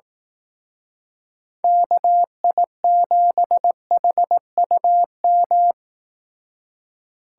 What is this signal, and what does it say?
Morse code, heard as a keyed tone, about 18 wpm, 700 Hz. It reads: KI7HUM
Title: Speed Racing - USA 2x3 Call Signs 15wpm